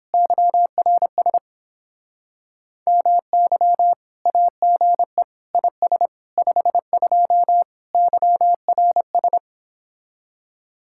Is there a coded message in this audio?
Yes: Morse code from 0.1 to 9.4 s, heard as a keyed tone, about 29 wpm, 700 Hz. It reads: YRS MY AGE IS 52 YRS